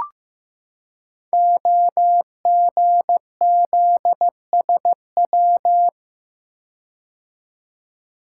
Morse code, heard as a keyed tone, about 15 wpm, 700 Hz. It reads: OGZSW
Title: Code Groups - 5 Letter:Numbers - 15wpm